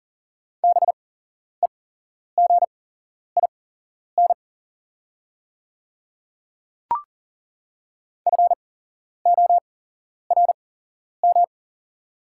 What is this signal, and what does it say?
Morse code, heard as a keyed tone, about 40 wpm, spaced as for 11 wpm, 700 Hz. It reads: BEGIN FORM
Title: 8x Character Spacing - Top 300 Words - 40wpm